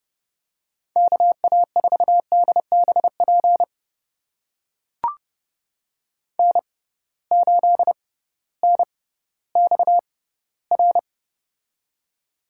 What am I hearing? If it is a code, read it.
Morse code, heard as a keyed tone, about 30 wpm, spaced as for 16 wpm, 700 Hz. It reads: KA4DBP N8NXR
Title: USA Call Signs - 6x Character Spacing - 30wpm